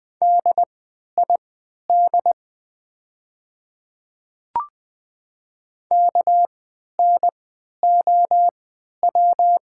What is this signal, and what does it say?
Morse code, heard as a keyed tone, about 20 wpm, spaced as for 11 wpm, 700 Hz. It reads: DID KNOW
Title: 3x Character Spacing - Top 100 Words - 20wpm